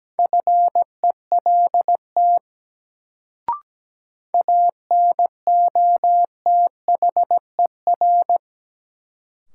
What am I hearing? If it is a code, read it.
Morse code, heard as a keyed tone, about 17 wpm, 700 Hz. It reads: FELT ANOTHER
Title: Sets of 5 Words Encore - New Word Drill - 17wpm